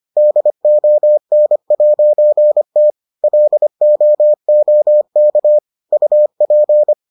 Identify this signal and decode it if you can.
Morse code, heard as a keyed tone, about 25 wpm, 600 Hz. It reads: DON'T LOOK UP